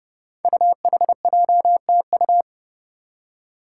Morse code, heard as a keyed tone, about 30 wpm, 700 Hz. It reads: UHJTU
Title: Five Challenging Letter-Number Combinations 30wpm